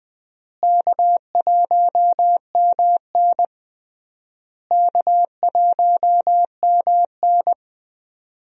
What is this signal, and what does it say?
Morse code, heard as a keyed tone, about 21 wpm, 700 Hz. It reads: K1MN K1MN
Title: USA Call Signs - 4 Characters - Repeated 3x - 20wpm